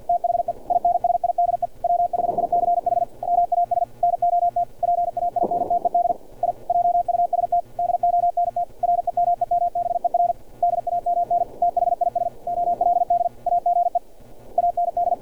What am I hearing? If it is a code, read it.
Morse code, heard as a keyed tone, about 50 wpm, 700 Hz, with radio noise. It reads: THE INSIDE WHICH MAS NOT PAINTED I JUST DONT REMEMBER DAMN THIS OLD AGE AND
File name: archived / CRSnet